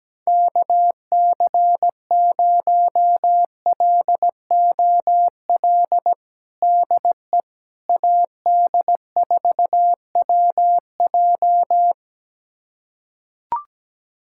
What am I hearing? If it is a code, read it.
Morse code, heard as a keyed tone, about 17 wpm, 700 Hz. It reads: KC0LOL DE AD4WJ
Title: USA Callsign DE Callsign 17wpm